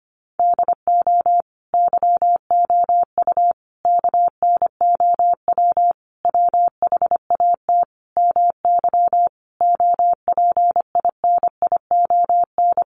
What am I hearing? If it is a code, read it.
Morse code, heard as a keyed tone, about 25 wpm, 700 Hz. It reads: DO YOU KNOW WHAT MY OPINION
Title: Sentences from Top 1200 Words 25wpm